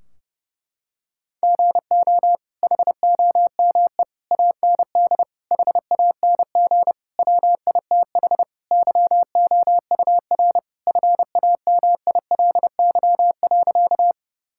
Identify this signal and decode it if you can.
Morse code, heard as a keyed tone, about 30 wpm, 700 Hz. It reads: GO HOME AND HANG WITH YOUR FAMILY.
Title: Sentences from Top 1600 Words 30wpm